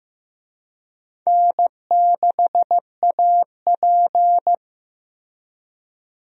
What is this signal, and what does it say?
Morse code, heard as a keyed tone, about 15 wpm, 700 Hz. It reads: N6AP
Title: USA Call Signs - 4 Characters - Repeated 2x - 15wpm